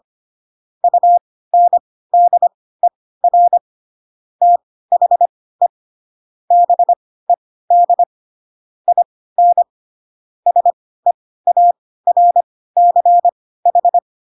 Morse code, heard as a keyed tone, about 25 wpm, spaced as for 16 wpm, 700 Hz. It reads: UNDER THE BED IN SEARCH
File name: Pinocchio / Pinocchio ch5 10 25 ik1hge